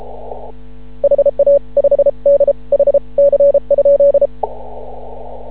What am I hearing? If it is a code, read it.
Morse code, heard as a keyed tone, about 33 wpm, 560 Hz. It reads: HA 5 D H C?
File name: CW / ha6oi 2